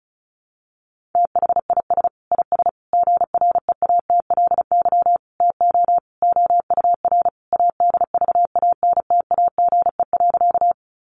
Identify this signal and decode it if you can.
Morse code, heard as a keyed tone, about 34 wpm, 700 Hz. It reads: THIS IS GREATLY TO OUR ADVANTAGE.
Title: Sentences from Top 1400 Words 35wpm